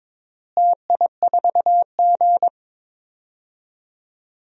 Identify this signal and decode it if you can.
Morse code, heard as a keyed tone, about 22 wpm, 700 Hz. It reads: TI4G
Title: DXCC-100 Call Signs - 4 Characters - Repeated 3x - 22wpm